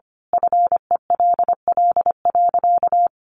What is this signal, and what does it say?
Morse code, heard as a keyed tone, about 25 wpm, 700 Hz. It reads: FELL.
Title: A Sound Of Thunder 25wpm